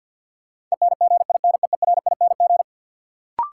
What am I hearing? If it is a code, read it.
Morse code, heard as a keyed tone, about 53 wpm, 700 Hz. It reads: ENGINEERING
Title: Sentences from Top 1500 Words - New Word Drill - 50wpm